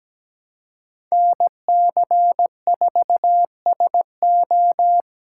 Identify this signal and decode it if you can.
Morse code, heard as a keyed tone, about 17 wpm, 700 Hz. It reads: NC4SO
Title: USA Callsign DE Callsign - 5 Characters - 17wpm